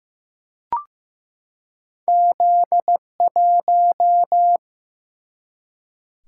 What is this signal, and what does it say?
Morse code, heard as a keyed tone, about 15 wpm, 700 Hz. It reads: Z1
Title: Single-Letter Single-Number 15wpm